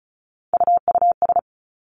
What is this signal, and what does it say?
Morse code, heard as a keyed tone, about 36 wpm, 700 Hz. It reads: UUS
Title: Three Challenging Letter-Number Combinations 35wpm